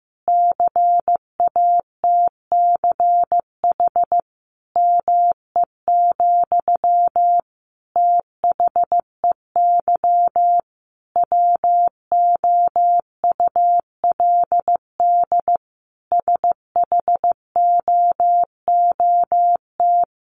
Morse code, heard as a keyed tone, about 15 wpm, 700 Hz. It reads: CATCH ME, THEY WOULD SHOOT
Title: Sentences from Top 1500 Words 15wpm